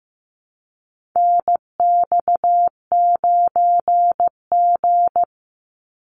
Morse code, heard as a keyed tone, about 15 wpm, 700 Hz. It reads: NX9G
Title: Call Signs Repeated 3x - 15wpm